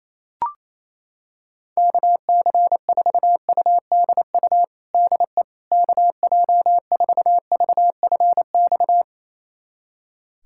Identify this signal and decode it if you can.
Morse code, heard as a keyed tone, about 28 wpm, 700 Hz. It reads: KC4UDU DE KJ4VFX
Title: USA Callsign DE Callsign - 6 Characters - 28wpm